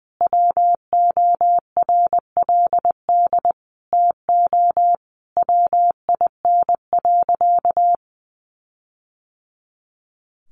Morse code, heard as a keyed tone, about 20 wpm, 700 Hz. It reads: WORLD TO WIN.
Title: Sentences from Top 1500 Words 20wpm